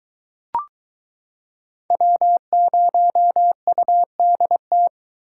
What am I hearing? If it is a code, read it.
Morse code, heard as a keyed tone, about 22 wpm, 700 Hz. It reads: W0UDT